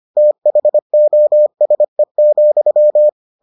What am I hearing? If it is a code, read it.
Morse code, heard as a keyed tone, about 25 wpm, 600 Hz. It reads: THOSE,